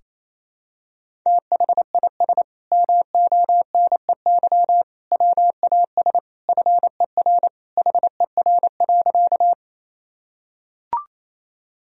Morse code, heard as a keyed tone, about 28 wpm, 700 Hz. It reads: THIS MONEY WAS FER HER.
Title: Sentences from Top 400 Words and Abbreviations 28wpm